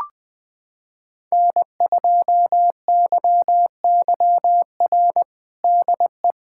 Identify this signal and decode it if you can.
Morse code, heard as a keyed tone, about 20 wpm, 700 Hz. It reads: N2YYR DE
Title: USA Callsign DE Callsign - 5 Characters - 20wpm